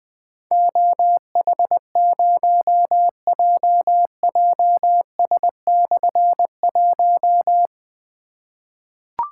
Morse code, heard as a keyed tone, about 20 wpm, 700 Hz. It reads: OH0JJS/1
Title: Difficult Call Signs 20wpm